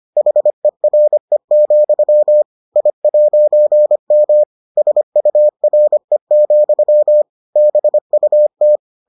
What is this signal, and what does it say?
Morse code, heard as a keyed tone, about 25 wpm, 600 Hz. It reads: HERE, I'M SURE, BUT